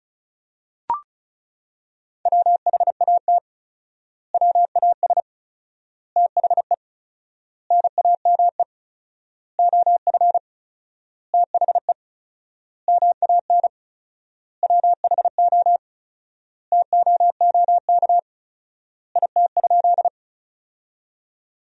Morse code, heard as a keyed tone, about 36 wpm, 700 Hz. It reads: WHAT WAS THE NAME OF THE MAN WHO TOOK IT?
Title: 4x Word-Spacing - Sentences from Top 300 Words - 35wpm